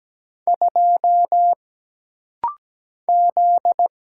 Morse code, heard as a keyed tone, about 17 wpm, 700 Hz. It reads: 2 Z
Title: Single Letter-Number Alt - Rapid-Fire 17wpm